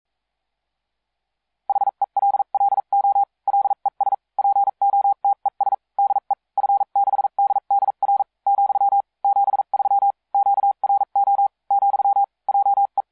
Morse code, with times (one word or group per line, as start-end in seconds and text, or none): 1.7-3.2 HELLO
3.5-4.1 LES
4.4-5.7 POTES
6.0-6.3 DE
6.6-8.2 F6DDR
8.5-9.0 ,
9.2-10.1 73
10.3-11.5 QRO
11.7-12.2 ,
12.5-13.0 JE